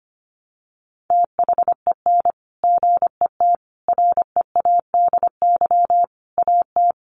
Morse code, heard as a keyed tone, about 25 wpm, 700 Hz. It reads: THEN GET READY AT
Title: Sentences from Top 400 Words 25wpm